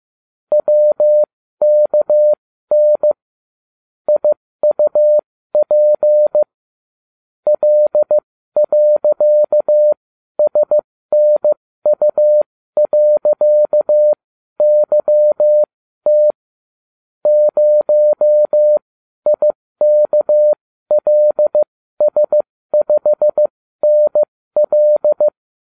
Morse code, heard as a keyed tone, about 15 wpm, spaced as for 12 wpm, 600 Hz. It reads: WKN IUP L.SNU.YT 0IKLS5NL